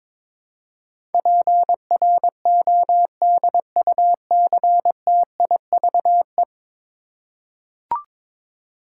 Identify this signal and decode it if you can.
Morse code, heard as a keyed tone, about 22 wpm, 700 Hz. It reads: PRODUCTIVE
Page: morse-code-ninja-practice-22wpm